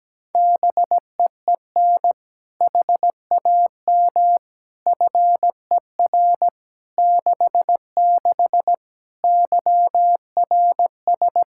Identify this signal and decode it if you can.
Morse code, heard as a keyed tone, about 17 wpm, 700 Hz. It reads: BEEN HAM FER 66 YRS
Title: Practice QSO - Exchange 3 - More Info - 17wpm